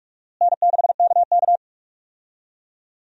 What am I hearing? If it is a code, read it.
Morse code, heard as a keyed tone, about 46 wpm, 700 Hz. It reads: N6KK
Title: USA Call Signs - 4 Characters - Repeated 2x - 45wpm